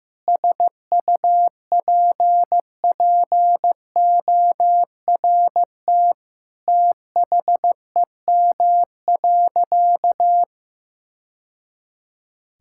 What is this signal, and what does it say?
Morse code, heard as a keyed tone, about 15 wpm, 700 Hz. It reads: SUPPORT THEM.
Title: Sentences from Top 1000 Words - Review Part 5 - 15wpm